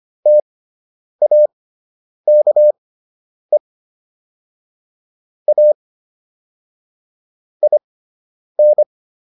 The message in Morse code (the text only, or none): TAKE A IN